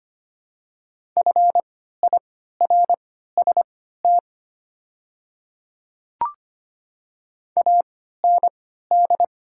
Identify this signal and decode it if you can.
Morse code, heard as a keyed tone, about 25 wpm, spaced as for 14 wpm, 700 Hz. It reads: FIRST AND
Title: 3x Character Spacing - Top 100 Words - 25wpm